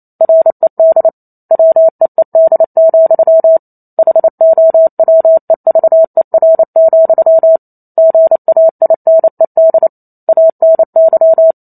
Morse code, heard as a keyed tone, about 28 wpm, 650 Hz. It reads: RED WEED, HOWEVER, GAINED ANY